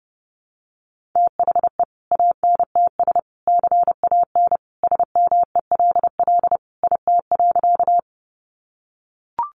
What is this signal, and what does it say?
Morse code, heard as a keyed tone, about 28 wpm, 700 Hz. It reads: THE ANTS CAN SMELL IT.
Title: Easy Phrases 30wpm